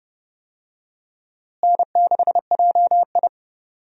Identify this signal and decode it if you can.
Morse code, heard as a keyed tone, about 30 wpm, 700 Hz. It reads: N6JI
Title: USA Call Signs - 4 Characters - Repeated 3x - 30wpm